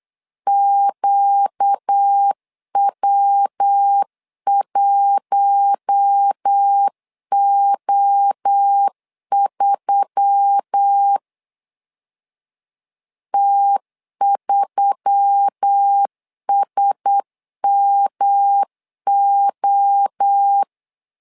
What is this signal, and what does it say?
Morse code, heard as a keyed tone, about 8 wpm, 800 Hz. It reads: QW1O3 T3SMO
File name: LEZIONI / 4 Q1S3 8wpm 3037